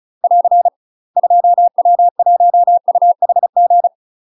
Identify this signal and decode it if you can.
Morse code, heard as a keyed tone, about 35 wpm, 700 Hz. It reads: <AR> 2W1UHG
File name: UK MASTER QSOs TRAINING / 35WPM / UK MASTER QSOs 0005